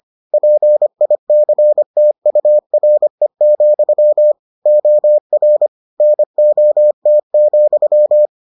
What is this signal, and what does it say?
Morse code, heard as a keyed tone, about 25 wpm, 600 Hz. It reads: PICTURE, OR NOT,